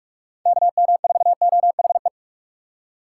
Morse code, heard as a keyed tone, about 59 wpm, spaced as for 52 wpm, 700 Hz. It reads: KM4OHE
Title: Speed Racing - USA 2x3 Call Signs 45wpm